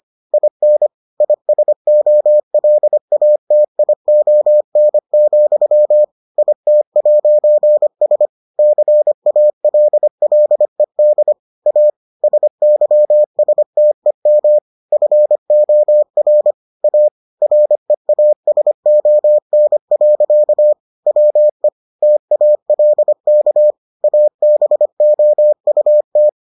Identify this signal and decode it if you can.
Morse code, heard as a keyed tone, about 25 wpm, 600 Hz. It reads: IN ISOLATION, IT'S CALLED A SYSTEM FOR A REASON. WE TALK ABOUT